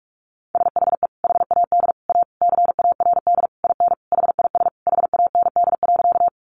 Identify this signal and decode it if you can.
Morse code, heard as a keyed tone, about 44 wpm, 700 Hz. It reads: SHE HAD A CARD IN HIS HAND.
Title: Sentences from Top 600 Words 45wpm